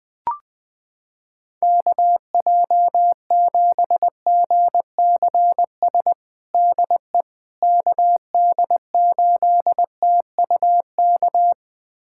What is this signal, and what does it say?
Morse code, heard as a keyed tone, about 21 wpm, 700 Hz. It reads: KJ7GCS DE KD8TUK
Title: USA Callsign DE Callsign 20wpm